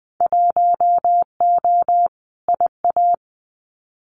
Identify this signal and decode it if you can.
Morse code, heard as a keyed tone, about 20 wpm, 700 Hz. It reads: 1O IA
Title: Winter Field Day Exchanges Encore 20wpm